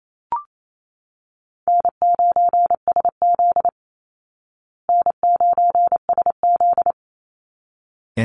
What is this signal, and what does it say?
Morse code, heard as a keyed tone, about 28 wpm, 700 Hz. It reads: N9SZ N9SZ
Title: Easy Call Signs Repeated 2x - 28wpm